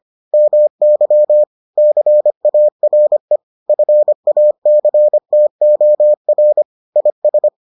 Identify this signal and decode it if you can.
Morse code, heard as a keyed tone, about 25 wpm, 600 Hz. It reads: MY CARE FACTOR IS